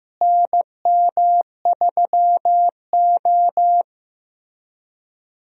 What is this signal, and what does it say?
Morse code, heard as a keyed tone, about 15 wpm, 700 Hz. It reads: NM3O
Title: USA Call Signs - 4 Characters - Repeated 3x - 15wpm